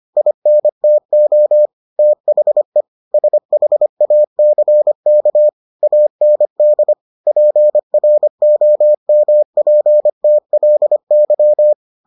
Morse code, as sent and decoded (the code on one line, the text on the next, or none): .. -. - --- / - .... / ... .... .- -.-. -.- / .- -. -.. / .--. .-. --- -- .--. - .-.. -.--
INTO THE SHACK AND PROMPTLY